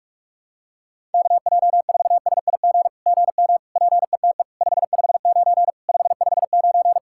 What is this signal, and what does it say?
Morse code, heard as a keyed tone, about 48 wpm, 700 Hz. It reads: KJ4SIG GM PETE 559 559
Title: SOTA QSO Practice 45wpm